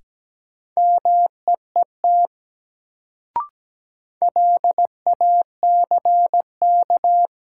Morse code, heard as a keyed tone, about 17 wpm, 700 Hz. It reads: MEET LACK
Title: Sentences from Top 1100 Words - New Word Drill Alt - 17wpm